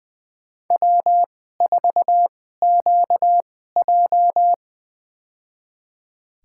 Morse code, heard as a keyed tone, about 20 wpm, spaced as for 14 wpm, 700 Hz. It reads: W4QJ
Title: USA Call Signs - 4 Characters - 2x Character Spacing - 20wpm